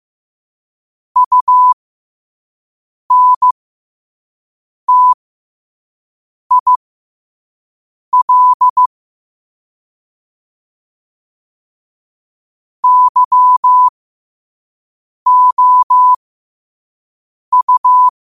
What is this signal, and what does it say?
Morse code, heard as a keyed tone, about 15 wpm, spaced as for 5 wpm, 1000 Hz. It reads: UNTIL YOU
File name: qotdfiles / 5wpm